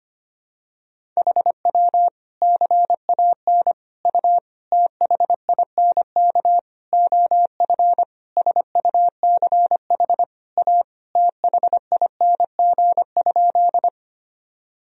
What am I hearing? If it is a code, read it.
Morse code, heard as a keyed tone, about 25 wpm, 700 Hz. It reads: HW CAN U THINK OF SUCH A THING?
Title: Sentences from Top 200 Words and Abbreviations 25wpm